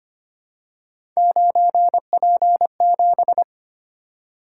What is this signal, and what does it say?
Morse code, heard as a keyed tone, about 25 wpm, 700 Hz. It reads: 9P7
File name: Three Letter-Number Combinations - 25wpm